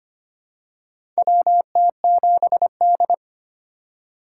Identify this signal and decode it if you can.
Morse code, heard as a keyed tone, about 25 wpm, 690 Hz. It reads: WT7D
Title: USA Call Signs - 4x Character Spacing - 25wpm